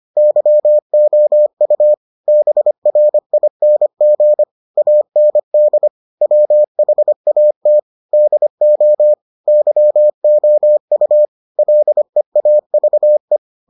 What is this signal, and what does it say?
Morse code, heard as a keyed tone, about 25 wpm, 600 Hz. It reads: YOU BRING AND WHAT DO YOU LEAVE